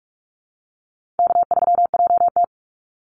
Morse code, heard as a keyed tone, about 46 wpm, 700 Hz. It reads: K3JT
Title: Easy Call Signs - 45wpm